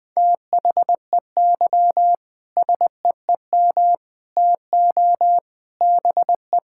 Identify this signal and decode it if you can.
Morse code, heard as a keyed tone, about 21 wpm, 700 Hz. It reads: THEY SEEM TO BE
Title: Sentences with 3-Word Phrases - Spoken Beginning - 20wpm